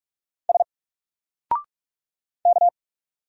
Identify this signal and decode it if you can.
Morse code, heard as a keyed tone, about 44 wpm, 700 Hz. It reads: S K
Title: Single Character (Letter:Number:Punctuation:Prosign) Alt 45wpm